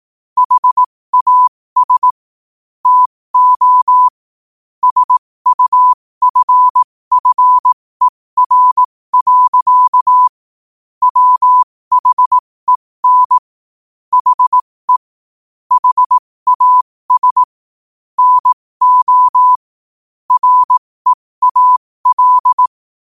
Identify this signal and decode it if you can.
Morse code, heard as a keyed tone, about 18 wpm, spaced as for 15 wpm, 1000 Hz. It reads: HAS TO SUFFER. WHEN HE HAS NO REAL
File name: qotdfiles / 15wpm